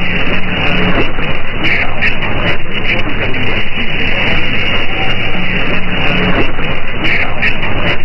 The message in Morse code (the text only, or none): none